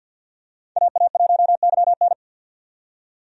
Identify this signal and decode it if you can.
Morse code, heard as a keyed tone, about 52 wpm, 700 Hz. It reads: AA1YN
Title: USA Call Signs - 5 Characters - 50wpm